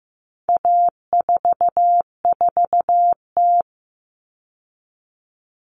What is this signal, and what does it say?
Morse code, heard as a keyed tone, about 15 wpm, 700 Hz. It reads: A44T
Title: All Cut Numbers - 4-Digit 15wpm